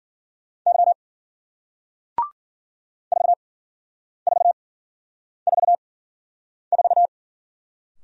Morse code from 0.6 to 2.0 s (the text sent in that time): Y